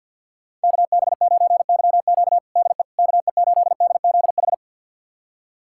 Morse code, heard as a keyed tone, about 51 wpm, 690 Hz. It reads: KB9YQ DE KE8DZH
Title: USA Callsign DE Callsign 50wpm